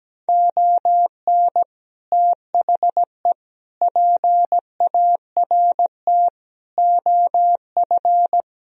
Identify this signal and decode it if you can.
Morse code, heard as a keyed tone, about 17 wpm, 700 Hz. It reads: ON THE PART OF
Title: Speed-Racing - Sets of 4 Words 17wpm